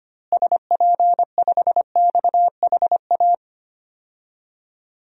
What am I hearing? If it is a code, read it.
Morse code, heard as a keyed tone, about 25 wpm, 700 Hz. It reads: SP5XHA
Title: DXCC-100 Call Signs - 6 Characters - Repeated 2x - 25wpm